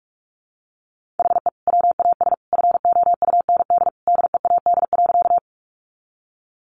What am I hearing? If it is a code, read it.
Morse code, heard as a keyed tone, about 44 wpm, 700 Hz. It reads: HE WAS FOUND DEAD.
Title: Sentences from Top 1500 Words 45wpm